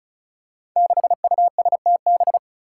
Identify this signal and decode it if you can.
Morse code, heard as a keyed tone, about 36 wpm, 700 Hz. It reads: 6USTB
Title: Five Challenging Letter-Number Combinations 35wpm